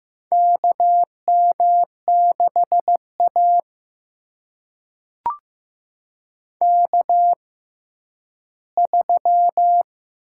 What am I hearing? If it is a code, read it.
Morse code, heard as a keyed tone, about 15 wpm, spaced as for 10 wpm, 700 Hz. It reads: KM6A K3
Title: USA Call Signs - 6x Character Spacing - 15wpm